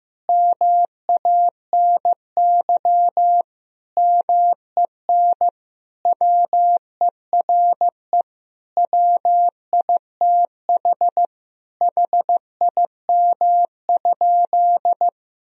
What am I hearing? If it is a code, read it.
Morse code, heard as a keyed tone, about 15 wpm, 700 Hz. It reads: MANY MEN WERE WITH HIM?
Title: Sentences from Top 1000 Words - Review Part 1 - 15wpm